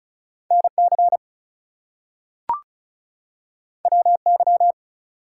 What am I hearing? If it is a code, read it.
Morse code, heard as a keyed tone, about 36 wpm, 700 Hz. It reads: NC WY
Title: US States and Canadian Provinces 35wpm